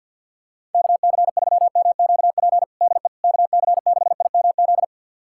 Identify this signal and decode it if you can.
Morse code, heard as a keyed tone, about 52 wpm, 700 Hz. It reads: KC3MQP DE KC6IMZ
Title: USA Callsign DE Callsign 50wpm